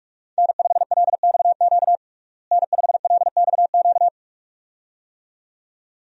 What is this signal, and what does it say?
Morse code, heard as a keyed tone, about 41 wpm, 700 Hz. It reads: N5LXQ N5LXQ
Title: USA Call Signs - 5 Characters - Repeated 2x - 45wpm